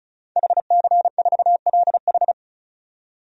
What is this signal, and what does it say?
Morse code, heard as a keyed tone, about 36 wpm, 700 Hz. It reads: HC4LH
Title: DXCC-100 Call Signs - 5 Characters - 35wpm